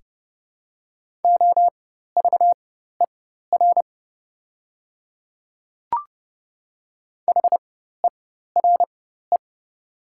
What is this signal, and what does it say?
Morse code, heard as a keyed tone, about 30 wpm, spaced as for 14 wpm, 700 Hz. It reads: OVER HERE
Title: 4x Character Spacing - Top 200 Words - 30wpm